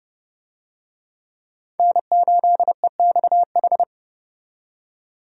Morse code, heard as a keyed tone, about 29 wpm, 700 Hz. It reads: N8EXH
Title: USA Call Signs - 5 Characters - 30wpm